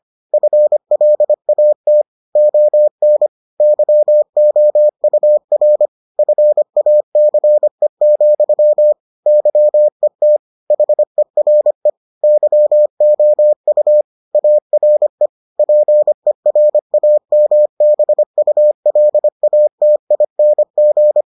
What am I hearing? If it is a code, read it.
Morse code, heard as a keyed tone, about 25 wpm, 600 Hz. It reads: FLAT ON YOUR FACE, YET HERE YOU ARE PERAMBULATING